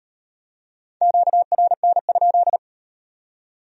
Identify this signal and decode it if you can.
Morse code, heard as a keyed tone, about 38 wpm, 700 Hz. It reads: QRN?